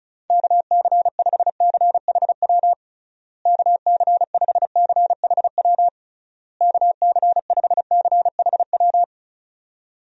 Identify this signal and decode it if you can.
Morse code, heard as a keyed tone, about 34 wpm, 700 Hz. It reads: KC5CHW KC5CHW KC5CHW
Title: USA Call Signs - 6 Characters - Repeated 3x - 35wpm